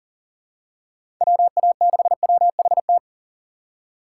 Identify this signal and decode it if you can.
Morse code, heard as a keyed tone, about 40 wpm, 700 Hz. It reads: WA6WHT
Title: USA Call Signs - 6 Characters - 4x Character Spacing - 40wpm